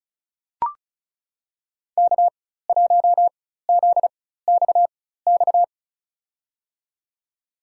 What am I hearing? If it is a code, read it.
Morse code, heard as a keyed tone, about 34 wpm, spaced as for 16 wpm, 700 Hz. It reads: K1ZXX
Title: USA Call Signs - 4x Character Spacing - 35wpm